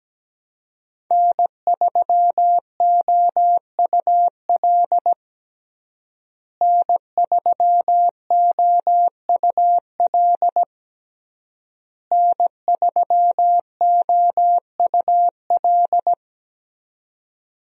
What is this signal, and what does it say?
Morse code, heard as a keyed tone, about 17 wpm, 700 Hz. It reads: N3OUL N3OUL N3OUL
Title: USA Call Signs - 5 Characters - Repeated 3x - 17wpm